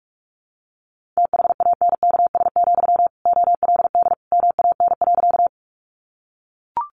Morse code, heard as a keyed tone, about 44 wpm, 700 Hz. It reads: THANKS, OLD MAN.
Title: Sentences from Top 600 Words 45wpm